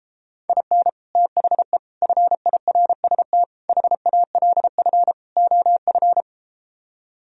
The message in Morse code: IN THE FIRST HALF OF

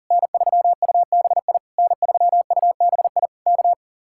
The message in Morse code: N3UBI N3UBI K

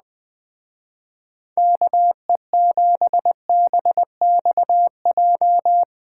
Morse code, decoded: KE7BXJ

